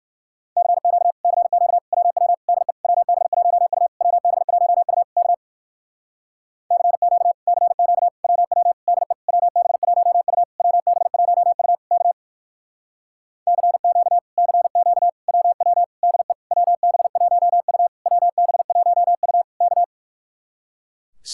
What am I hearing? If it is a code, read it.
Morse code, heard as a keyed tone, about 65 wpm, 700 Hz. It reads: CQ CQ WW DE WB1U WB1U K CQ CQ WW DE WB1U WB1U K CQ CQ WW DE WB1U WB1U K